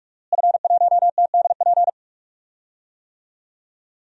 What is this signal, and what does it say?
Morse code, heard as a keyed tone, about 43 wpm, 700 Hz. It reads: F1TDP
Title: DXCC-100 Call Signs - 5 Characters - Repeated 2x - 45wpm